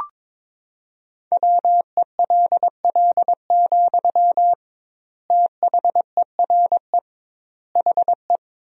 Morse code, heard as a keyed tone, about 22 wpm, 700 Hz. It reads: WELL, THERE HE
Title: 2x Word-Spacing - Sentences from Top 500 Words - 22wpm